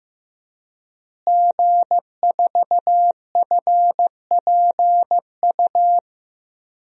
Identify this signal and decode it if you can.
Morse code, heard as a keyed tone, about 15 wpm, 700 Hz. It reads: G4FPU
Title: DXCC-100 Call Signs - 5 Characters - Repeated 2x - 15wpm